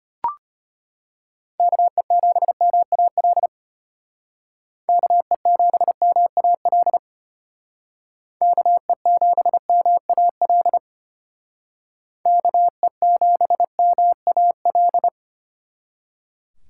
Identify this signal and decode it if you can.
Morse code, heard as a keyed tone, about 30 wpm, 700 Hz. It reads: KE7MAL KE7MAL KE7MAL KE7MAL
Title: Speed Racing - USA 2x3 Call Signs 25wpm